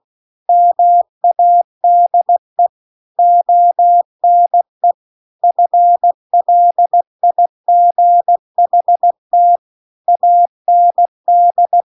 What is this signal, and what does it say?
Morse code, heard as a keyed tone, about 16 wpm, 700 Hz. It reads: MADE ONE FLIGHT AND
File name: 16WPM